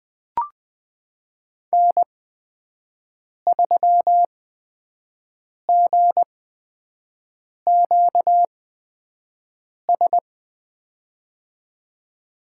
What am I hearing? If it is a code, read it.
Morse code, heard as a keyed tone, about 20 wpm, spaced as for 5 wpm, 700 Hz. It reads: N3GQS
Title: USA Call Signs - 8x Character Spacing - 20wpm